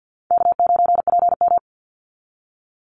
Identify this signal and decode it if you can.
Morse code, heard as a keyed tone, about 50 wpm, 700 Hz. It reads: K9PM
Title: NA SOTA Chaser Callsigns 50wpm